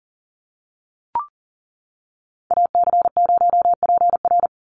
Morse code, heard as a keyed tone, about 40 wpm, 700 Hz. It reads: AC0PR
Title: NA SOTA Activator Callsigns 40wpm